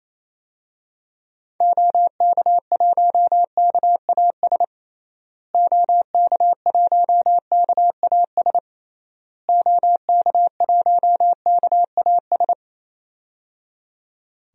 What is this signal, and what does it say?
Morse code, heard as a keyed tone, about 28 wpm, 700 Hz. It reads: OK1KAS OK1KAS OK1KAS
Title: DXCC-100 Call Signs - 6 Characters - Repeated 3x - 28wpm